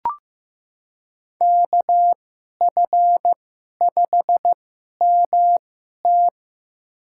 Morse code, heard as a keyed tone, about 15 wpm, spaced as for 11 wpm, 700 Hz. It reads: KF5MT
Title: USA Call Signs - 5 Characters - 2x Character Spacing - 15wpm